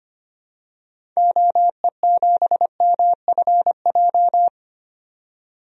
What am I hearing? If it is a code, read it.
Morse code, heard as a keyed tone, about 25 wpm, 700 Hz. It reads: OE7MFJ
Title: DXCC-100 Call Signs - 6 Characters - 25wpm